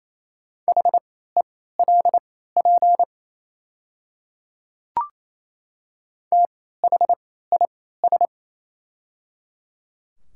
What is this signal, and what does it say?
Morse code, heard as a keyed tone, about 28 wpm, spaced as for 16 wpm, 700 Hz. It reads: HELP THIS
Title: 3x Character Spacing - Top 500 Words - 28wpm